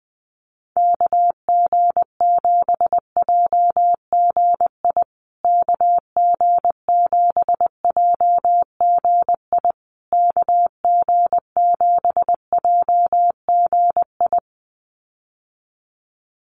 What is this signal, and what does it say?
Morse code, heard as a keyed tone, about 20 wpm, 700 Hz. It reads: KG7JGI KG7JGI KG7JGI
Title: USA Call Signs Repeated 3x 20wpm